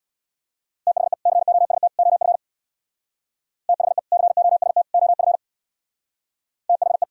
Morse code, heard as a keyed tone, about 61 wpm, 700 Hz. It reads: THE COST OF THE COST OF THE
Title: Speed-Racing - Sets of 3 Words 50wpm